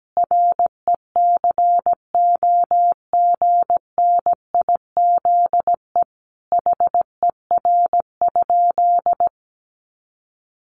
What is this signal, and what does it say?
Morse code, heard as a keyed tone, about 17 wpm, 700 Hz. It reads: RECOGNIZE HER?
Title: Sentences from Top 1400 Words 17wpm